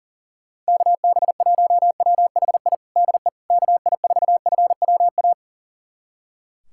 Morse code, heard as a keyed tone, about 40 wpm, 700 Hz. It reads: KB1WHI DE KI4FWA
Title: USA Callsign DE Callsign - 6 Characters - 40wpm